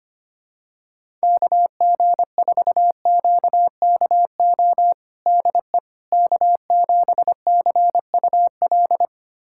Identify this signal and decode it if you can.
Morse code, heard as a keyed tone, about 25 wpm, 700 Hz. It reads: KG4QKO DE K7CUL
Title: USA Callsign DE Callsign 25wpm